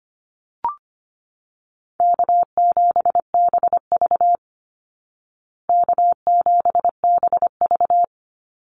Morse code, heard as a keyed tone, about 25 wpm, 700 Hz. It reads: K7BV K7BV
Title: Easy Call Signs Repeated 2x - 25wpm